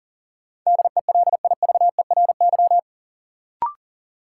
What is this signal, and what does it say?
Morse code, heard as a keyed tone, about 40 wpm, 700 Hz. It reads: DELIVERY
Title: Sentences from Top 1500 Words - New Word Drill Alt - 40wpm